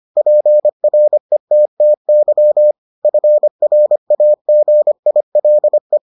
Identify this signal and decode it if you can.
Morse code, heard as a keyed tone, about 25 wpm, 600 Hz. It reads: PRETTY FRAGILE